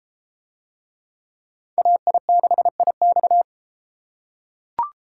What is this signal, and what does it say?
Morse code, heard as a keyed tone, about 32 wpm, 700 Hz. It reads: AI6IX